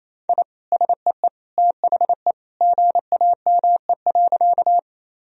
Morse code, heard as a keyed tone, about 28 wpm, 700 Hz. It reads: I SEE THE GAME.
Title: Sentences from Top 1000 Words - Review Part 3 - 28wpm